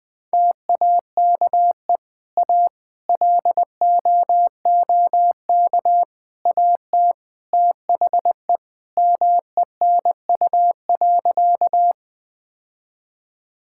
Morse code, heard as a keyed tone, about 21 wpm, 700 Hz. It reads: TAKE A LOOK AT THE MENU.
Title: Sentences with 4-Word Phrases - 20wpm